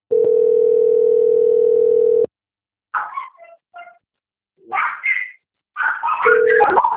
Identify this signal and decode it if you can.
Ringback tone (North America), 440 + 480 Hz.